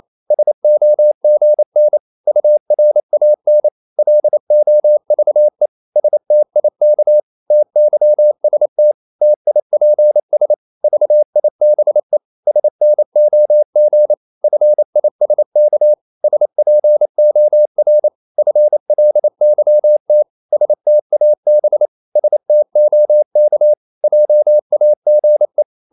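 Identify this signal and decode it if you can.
Morse code, heard as a keyed tone, about 28 wpm, 600 Hz. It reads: SOGN URAN LOVE STIK TYST TIPS VIBE SNOG FISK SPOR FLYT STAB STOK JAGE
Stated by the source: Danish short words sent in Morse at 28 wpm, from CW med Gnister.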